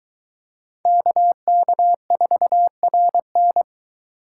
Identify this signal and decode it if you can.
Morse code, heard as a keyed tone, about 22 wpm, 700 Hz. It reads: KK4RN